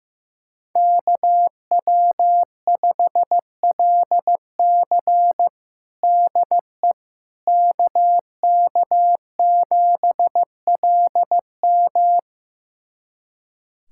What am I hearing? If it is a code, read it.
Morse code, heard as a keyed tone, about 15 wpm, 700 Hz. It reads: KW5LC DE KK7LM
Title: USA Callsign DE Callsign - 5 Characters - 15wpm